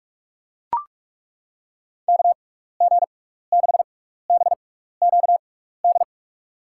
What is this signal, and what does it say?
Morse code, heard as a keyed tone, about 41 wpm, spaced as for 15 wpm, 700 Hz. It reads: KG6BQD
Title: USA Call Signs - 6x Character Spacing - 45wpm